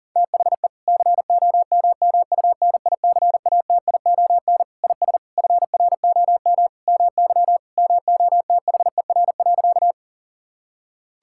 Morse code, heard as a keyed tone, about 40 wpm, 700 Hz. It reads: THE COMMUNICATION IS FROM MY MOTHER.
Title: Sentences from Top 1000 Words - Review Part 6 - 40wpm